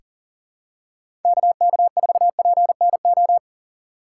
Morse code, heard as a keyed tone, about 37 wpm, 700 Hz. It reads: KK4PNO
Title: USA Call Signs - 6 Characters - 2x Character Spacing - 40wpm